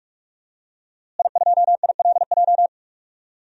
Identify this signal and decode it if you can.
Morse code, heard as a keyed tone, about 46 wpm, 700 Hz. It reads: I2ILJ
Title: DXCC-100 Call Signs - 5 Characters - Repeated 2x - 45wpm